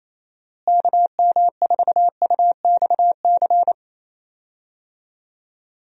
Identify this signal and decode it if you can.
Morse code, heard as a keyed tone, about 28 wpm, 700 Hz. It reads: KM4UXC
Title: USA Call Signs - 6 Characters - Repeated 2x - 28wpm